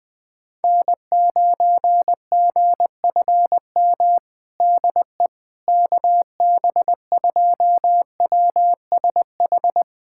Morse code, heard as a keyed tone, about 20 wpm, 700 Hz. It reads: N9GFM DE KB2WSH
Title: USA Callsign DE Callsign 20wpm